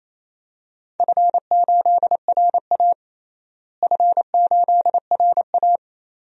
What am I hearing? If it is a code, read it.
Morse code, heard as a keyed tone, about 28 wpm, 700 Hz. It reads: F8RA F8RA
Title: DXCC-100 Call Signs - 4 Characters - Repeated 3x - 28wpm